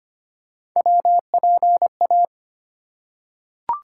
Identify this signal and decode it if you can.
Morse code, heard as a keyed tone, about 25 wpm, 700 Hz. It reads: WPA
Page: morse-code-ninja-practice-25wpm